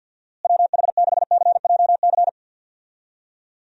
Morse code, heard as a keyed tone, about 52 wpm, 700 Hz. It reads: WH6CJC
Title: USA Call Signs - 6 Characters - Repeated 3x - 50wpm